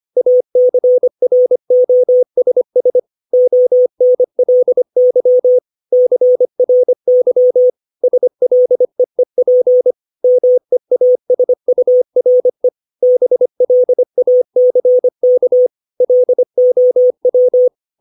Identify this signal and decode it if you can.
Morse code, heard as a keyed tone, about 25 wpm, 500 Hz. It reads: ACROSS ONLY CRY SLEEP MEASURE BLACK LOW
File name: 632 words / 632 Words 25WPM